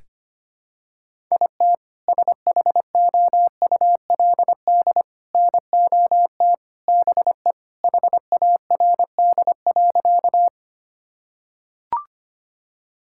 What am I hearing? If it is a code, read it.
Morse code, heard as a keyed tone, about 25 wpm, 700 Hz. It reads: IT SHOULD NOT BE HARD.
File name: practice / 25wpm / Sentences with 2-Word Phrases - 25wpm